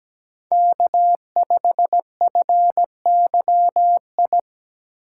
Morse code, heard as a keyed tone, about 21 wpm, 700 Hz. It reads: K5FYI
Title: Speed Racing - USA 1x3 Call Signs 17wpm